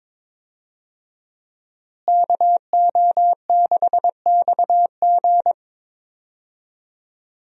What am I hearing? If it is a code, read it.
Morse code, heard as a keyed tone, about 21 wpm, 700 Hz. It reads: KO6XG
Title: USA Call Signs - 5 Characters - Repeated 3x - 22wpm